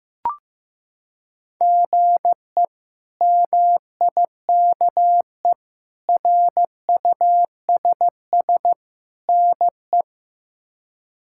Morse code, heard as a keyed tone, about 16 wpm, 700 Hz. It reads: GE MIKE RUSS NE
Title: K1USN SST QSO Practice - 15wpm